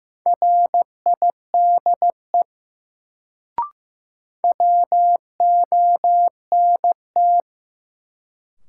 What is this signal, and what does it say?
Morse code, heard as a keyed tone, about 16 wpm, 700 Hz. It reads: RIDE WONT
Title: Sentences from Top 1000 Words - Review Part 6 - New Word Drill - 15wpm